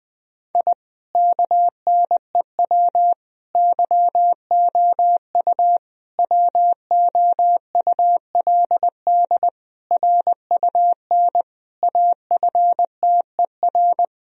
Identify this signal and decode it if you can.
Morse code, heard as a keyed tone, about 20 wpm, 700 Hz. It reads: I KNEW YOU WOULD RUN AFTER
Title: Sentences from Top 1000 Words - Review Part 1 - 20wpm